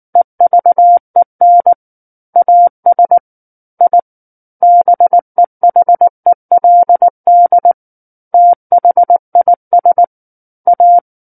Morse code, heard as a keyed tone, about 19 wpm, 700 Hz. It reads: EVEN AS I BEHELD THIS A
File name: Book One / 10-Chapter 10 - 19 WPM